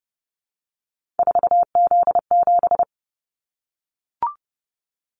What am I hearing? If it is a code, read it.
Morse code, heard as a keyed tone, about 31 wpm, 700 Hz. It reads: 4Z7
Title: Callsign Prefixes for Top 50 DXCC - 30wpm